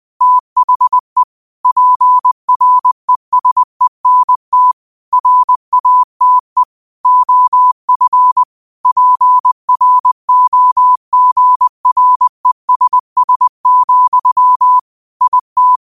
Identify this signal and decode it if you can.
Morse code, heard as a keyed tone, about 20 wpm, 1000 Hz. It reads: THE PRESENT RATE OF PROGRESS, IT